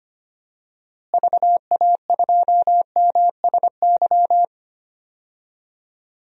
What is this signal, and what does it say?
Morse code, heard as a keyed tone, about 25 wpm, 700 Hz. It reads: VA2MSY
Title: DXCC-100 Call Signs - 6 Characters - Repeated 3x - 25wpm